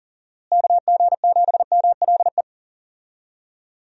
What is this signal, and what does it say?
Morse code, heard as a keyed tone, about 40 wpm, 700 Hz. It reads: KG7MLE
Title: USA Call Signs - 6 Characters - Repeated 2x - 40wpm